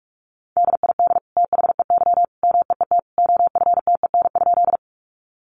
Morse code, heard as a keyed tone, about 44 wpm, 700 Hz. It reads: DID THEY MEET OFTEN?
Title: Sentences from Top 1100 Words 45wpm